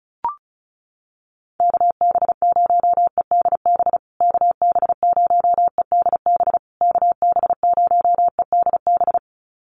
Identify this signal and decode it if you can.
Morse code, heard as a keyed tone, about 34 wpm, 700 Hz. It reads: KB0EDB KB0EDB KB0EDB
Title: USA Call Signs Repeated 3x 35wpm